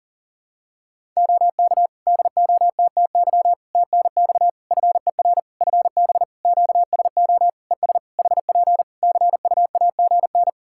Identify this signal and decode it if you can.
Morse code, heard as a keyed tone, about 40 wpm, 700 Hz. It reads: OK DOTTY TNX FER FB QSO ES HP CUAGN
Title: Practice QSO - Exchange 4 - Ending - 40wpm